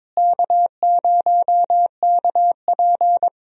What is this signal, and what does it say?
Morse code, heard as a keyed tone, about 22 wpm, 700 Hz. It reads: K0KP
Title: USA Call Signs - 4 Characters - Repeated 3x - 22wpm